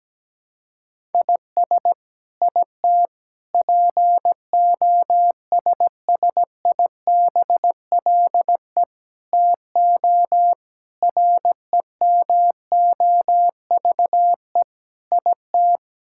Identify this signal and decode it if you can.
Morse code, heard as a keyed tone, about 17 wpm, 700 Hz. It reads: IS IT POSSIBLE TO REMOVE IT
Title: Sentences from Top 1000 Words - Review Part 5 - 17wpm